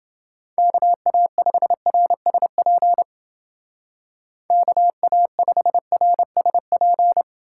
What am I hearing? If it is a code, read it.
Morse code, heard as a keyed tone, about 29 wpm, 700 Hz. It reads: KA5RSP KA5RSP